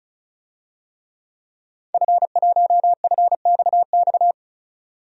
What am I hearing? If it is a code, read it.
Morse code, heard as a keyed tone, about 34 wpm, 700 Hz. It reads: F1FXX
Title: DXCC-100 Call Signs - 5 Characters - 35wpm